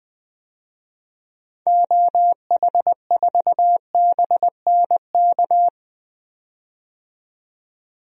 Morse code, heard as a keyed tone, about 20 wpm, 700 Hz. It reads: OH4BNK
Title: DXCC-100 Call Signs - 6 Characters - 20wpm